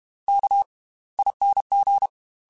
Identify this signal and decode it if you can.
Morse code, heard as a keyed tone, about 32 wpm, spaced as for 21 wpm, 800 Hz. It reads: KING
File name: Slow ing @30